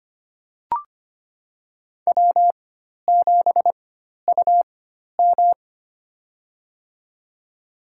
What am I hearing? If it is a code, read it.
Morse code, heard as a keyed tone, about 25 wpm, spaced as for 12 wpm, 700 Hz. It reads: W7UM
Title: USA Call Signs - 4 Characters - 4x Character Spacing - 25wpm